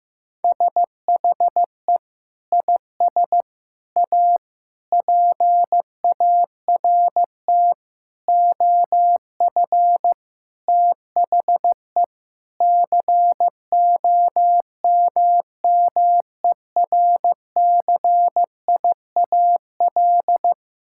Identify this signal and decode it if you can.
Morse code, heard as a keyed tone, about 15 wpm, 700 Hz. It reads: SHE IS A PART OF THE COMMERCIAL
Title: Sentences from Top 1000 Words - Review Part 5 - 15wpm